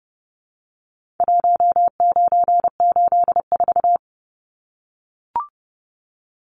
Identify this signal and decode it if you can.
Morse code, heard as a keyed tone, about 30 wpm, 700 Hz. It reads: 1984
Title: Numbers - Contest Style - 4-Digit 30wpm